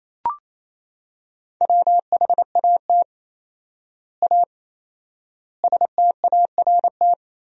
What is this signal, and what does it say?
Morse code, heard as a keyed tone, about 28 wpm, 700 Hz. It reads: WHAT A START
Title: 4x Word-Spacing - Sentences from Top 300 Words - 28wpm